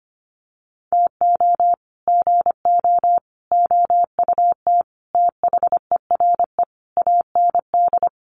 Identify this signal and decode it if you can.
Morse code, heard as a keyed tone, about 25 wpm, 700 Hz. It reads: TO GO OUT THERE AND
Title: Sets of 5 Words Encore2 25wpm